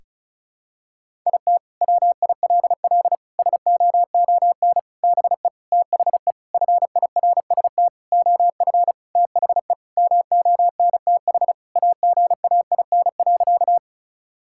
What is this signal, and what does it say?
Morse code, heard as a keyed tone, about 36 wpm, 690 Hz. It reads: IT WILL SOON BE THE FIRST OF THE MONTH AGAIN.
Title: Sentences from Top 1000 Words - Review Part 5 - 35wpm